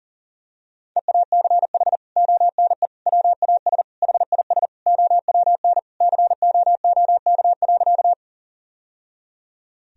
Morse code, heard as a keyed tone, about 40 wpm, 700 Hz. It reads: EACH ONE WAS HIS OWN COOK.
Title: Sentences from Top 1000 Words - Review Part 4 - 40wpm